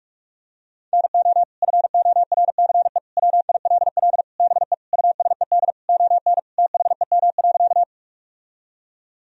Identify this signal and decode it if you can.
Morse code, heard as a keyed tone, about 46 wpm, 700 Hz. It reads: NO FORCE WILL BE USED ON THEM.
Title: Sentences from Top 1000 Words - Review Part 3 - 45wpm